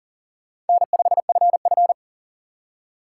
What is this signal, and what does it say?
Morse code, heard as a keyed tone, about 51 wpm, 700 Hz. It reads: N5FF